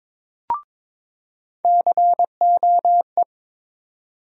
Morse code, heard as a keyed tone, about 22 wpm, 700 Hz. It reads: COE
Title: USA Call Sign Postfixes - 3 Letters 22wpm